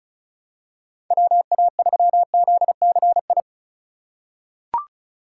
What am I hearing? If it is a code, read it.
Morse code, heard as a keyed tone, about 36 wpm, 700 Hz. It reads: WA3ZCI
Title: USA Call Signs - 4x Character Spacing - 35wpm